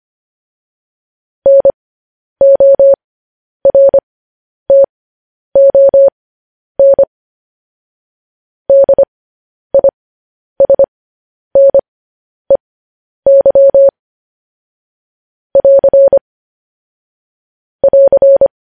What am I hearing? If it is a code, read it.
Morse code, heard as a keyed tone, about 25 wpm, spaced as for 10 wpm, 550 Hz. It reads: NORTON DISNEY <AR> <AR>
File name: podcast / News headlines-1x-25x10